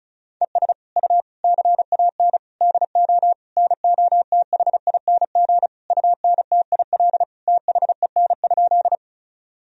Morse code, heard as a keyed tone, about 36 wpm, 700 Hz. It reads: ES U CAN DO NOTHING UNTIL THEN?
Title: Sentences from Top 500 Words and Abbreviations 35wpm